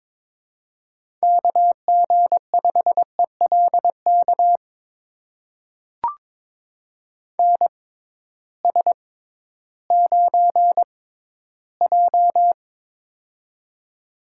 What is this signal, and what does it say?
Morse code, heard as a keyed tone, about 22 wpm, spaced as for 13 wpm, 700 Hz. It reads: KG5ELK NS9J
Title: USA Call Signs - 6x Character Spacing - 22wpm